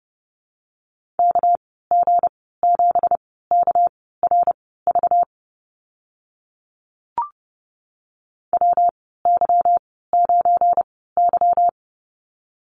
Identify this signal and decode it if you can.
Morse code, heard as a keyed tone, about 30 wpm, spaced as for 17 wpm, 700 Hz. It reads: KG7KRV WY9Y
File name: Lesson 37g - Letter x - Call Signs 30wpm